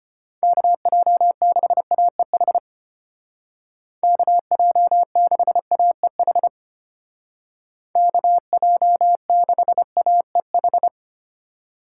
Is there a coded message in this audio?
Yes: Morse code from 0.4 to 10.9 s, heard as a keyed tone, about 32 wpm, 700 Hz. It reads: KJ6AEH KJ6AEH KJ6AEH